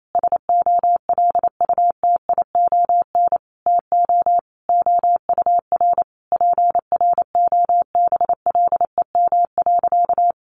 Morse code, heard as a keyed tone, about 28 wpm, 700 Hz. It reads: SOLUTION TO OUR PROBLEM.
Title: Sentences from Top 1300 Words 28wpm